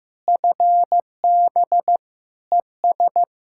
Morse code, heard as a keyed tone, about 16 wpm, 700 Hz. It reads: FB ES
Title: Practice QSO - Exchange 2 - Info - 15wpm